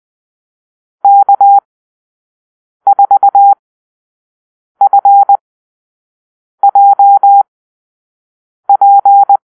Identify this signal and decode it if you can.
Morse code, heard as a keyed tone, about 20 wpm, spaced as for 6 wpm, 800 Hz. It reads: K4FJP
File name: arrl / Call-Sign Drill-5